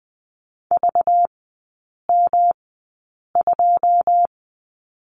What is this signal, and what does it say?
Morse code, heard as a keyed tone, about 20 wpm, spaced as for 8 wpm, 700 Hz. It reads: VM2
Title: 198 Lesson 32e - Letter k - Three Characters 20wpm